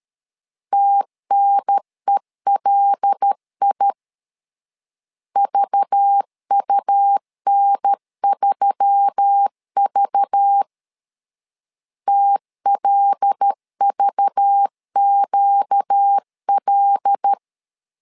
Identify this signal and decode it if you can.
Morse code, heard as a keyed tone, about 13 wpm, 800 Hz. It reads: TNELI VUN3V TLVQL
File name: LEZIONI / 5 LBVC / 5 LBV 12wpm 1855